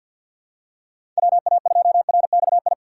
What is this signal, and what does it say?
Morse code, heard as a keyed tone, about 48 wpm, 700 Hz. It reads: WA2RXI
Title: USA Call Signs - 6 Characters - 3x Character Spacing - 50wpm